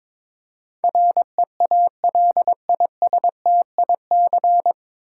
Morse code, heard as a keyed tone, about 22 wpm, 700 Hz. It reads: REALISTIC
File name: 9 Letter Words - First 3 Letters Given - 22wpm